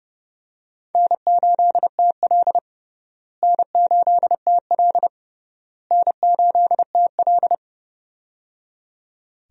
Morse code, heard as a keyed tone, about 30 wpm, 700 Hz. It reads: N8TL N8TL N8TL
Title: USA Call Signs - 4 Characters - Repeated 3x - 30wpm